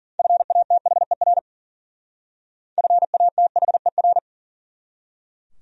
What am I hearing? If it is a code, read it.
Morse code, heard as a keyed tone, about 49 wpm, 700 Hz. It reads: FATHER FATHER
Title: Speed-Racing Top 500 Words 40wpm